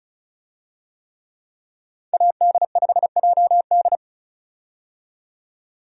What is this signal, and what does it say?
Morse code, heard as a keyed tone, about 36 wpm, 700 Hz. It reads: AD5JD
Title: USA Call Signs - 5 Characters - 35wpm